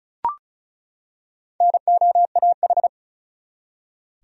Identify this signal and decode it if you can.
Morse code, heard as a keyed tone, about 36 wpm, 700 Hz. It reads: NOAH
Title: USA Common Names - Max 4 Letters - 35wpm